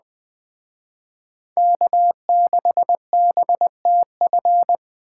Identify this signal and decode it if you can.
Morse code, heard as a keyed tone, about 20 wpm, 700 Hz. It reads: K6BTF
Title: Speed Racing - USA 1x3 Call Signs 17wpm